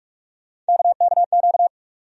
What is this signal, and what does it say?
Morse code, heard as a keyed tone, about 43 wpm, 700 Hz. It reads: KKQ
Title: USA Call Sign Postfixes - 3 Letters 45wpm